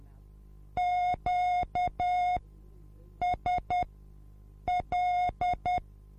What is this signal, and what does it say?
Morse code, heard as a keyed tone, about 10 wpm, spaced as for 7 wpm, 720 Hz. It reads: QSL